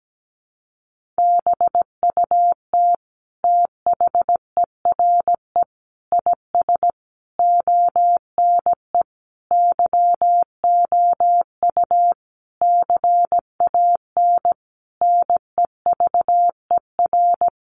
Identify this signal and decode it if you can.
Morse code, heard as a keyed tone, about 17 wpm, 700 Hz. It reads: BUT THERE IS ONE YOU CAN NEVER
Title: Sentences from Top 1200 Words 17wpm